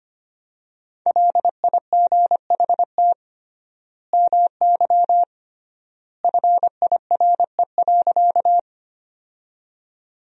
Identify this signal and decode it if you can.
Morse code, heard as a keyed tone, about 25 wpm, 700 Hz. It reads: LIGHT MY FIRE.
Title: 3x Word-Spacing - Sentences from Top 400 Words - 25wpm